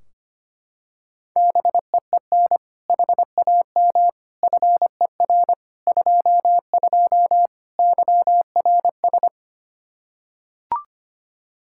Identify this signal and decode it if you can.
Morse code, heard as a keyed tone, about 25 wpm, 700 Hz. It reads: BEEN HAM FER 22 YRS
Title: Practice QSO - Exchange 3 - More Info - 25wpm